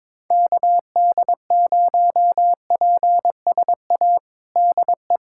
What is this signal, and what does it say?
Morse code, heard as a keyed tone, about 22 wpm, 700 Hz. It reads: KD0PSA DE
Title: USA Callsign DE Callsign - 6 Characters - 22wpm